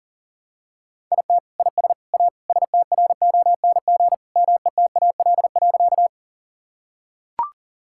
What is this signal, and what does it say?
Morse code, heard as a keyed tone, about 40 wpm, 700 Hz. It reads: IT IS A STRONG METAL.
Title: Sentences from Top 1000 Words - Review Part 6 - 40wpm